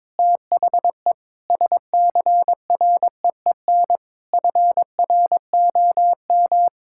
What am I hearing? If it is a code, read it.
Morse code, heard as a keyed tone, about 22 wpm, 700 Hz. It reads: THE SCREEN FROM
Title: Sentences from Top 1000 Words - Review Part 6 - 22wpm